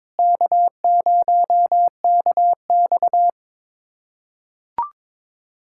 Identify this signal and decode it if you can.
Morse code, heard as a keyed tone, about 21 wpm, 700 Hz. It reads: K0KX
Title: K1USN SST QSO Practice - 22wpm